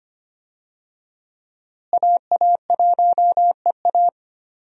Morse code, heard as a keyed tone, about 25 wpm, 700 Hz. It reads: AA1EA
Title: USA Call Signs - 5 Characters - Repeated 3x - 25wpm